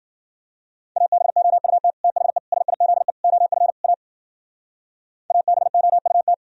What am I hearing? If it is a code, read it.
Morse code, heard as a keyed tone, about 51 wpm, 700 Hz. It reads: ABOUT THE SIZE OF A ABOUT